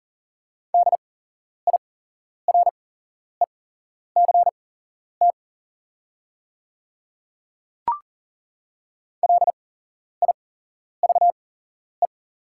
Morse code, heard as a keyed tone, about 40 wpm, spaced as for 11 wpm, 700 Hz. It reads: DIRECT LIVE
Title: 8x Character Spacing - Top 500 Words - 40wpm